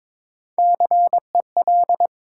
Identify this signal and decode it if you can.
Morse code, heard as a keyed tone, about 22 wpm, 700 Hz. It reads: CEL
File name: 9 Letter Words - First 3 Letters Given - 22wpm